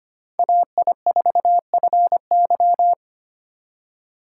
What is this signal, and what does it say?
Morse code, heard as a keyed tone, about 31 wpm, 700 Hz. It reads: AI4FY
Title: Speed Racing - USA 2x2 Call Signs 25wpm